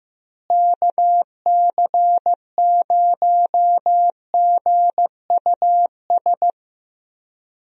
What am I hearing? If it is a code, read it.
Morse code, heard as a keyed tone, about 15 wpm, 700 Hz. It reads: KC0GUS
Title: USA Call Signs - 6 Characters - Repeated 2x - 15wpm